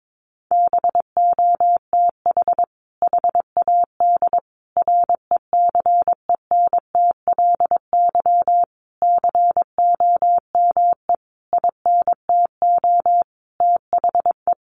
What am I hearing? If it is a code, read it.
Morse code, heard as a keyed tone, about 22 wpm, 700 Hz. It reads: BOTH HAD RECENTLY COME INTO THE
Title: Sentences from Top 1200 Words 22wpm